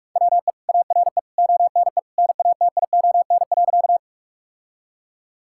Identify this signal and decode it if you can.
Morse code, heard as a keyed tone, about 46 wpm, 700 Hz. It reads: WE ARE ONE NATION.
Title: Sentences from Top 1000 Words - Review Part 6 - 45wpm